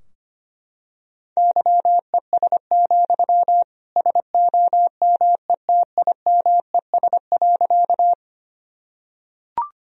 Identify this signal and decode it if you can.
Morse code, heard as a keyed tone, about 25 wpm, 700 Hz. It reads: YES, SOMETIMES.
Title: Sentences from Top 1000 Words - Review Part 3 - 25wpm